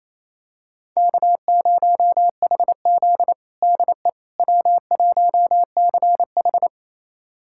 Morse code, heard as a keyed tone, about 28 wpm, 700 Hz. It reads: K0HZ DE W1CH
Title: USA Callsign DE Callsign - 4 Characters - 28wpm